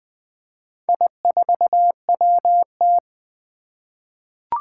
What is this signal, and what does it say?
Morse code, heard as a keyed tone, about 20 wpm, 700 Hz. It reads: I4WT
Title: Four Challenging Letter-Number Combinations 20wpm